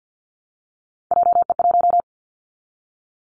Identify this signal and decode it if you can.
Morse code, heard as a keyed tone, about 50 wpm, 700 Hz. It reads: JE1
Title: Callsign Prefixes for Top 25 DXCC - 50wpm